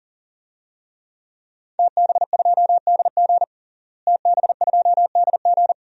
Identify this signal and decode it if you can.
Morse code, heard as a keyed tone, about 40 wpm, 700 Hz. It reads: TB2DG TB2DG
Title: DXCC-100 Call Signs - 5 Characters - Repeated 2x - 40wpm